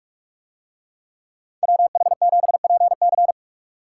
Morse code, heard as a keyed tone, about 44 wpm, 700 Hz. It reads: WH7PC